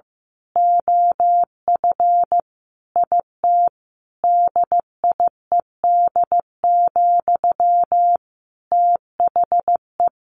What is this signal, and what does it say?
Morse code, heard as a keyed tone, about 15 wpm, 700 Hz. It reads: OF IT DIED, THE
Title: A Sound Of Thunder 15wpm